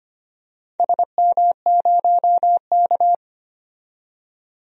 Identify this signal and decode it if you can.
Morse code, heard as a keyed tone, about 25 wpm, 700 Hz. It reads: SM0K
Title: DXCC-100 Call Signs - 4 Characters - 25wpm